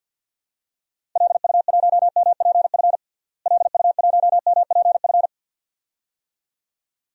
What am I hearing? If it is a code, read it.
Morse code, heard as a keyed tone, about 52 wpm, 700 Hz. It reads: LU1MPF LU1MPF
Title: DXCC-100 Call Signs - 6 Characters - Repeated 2x - 50wpm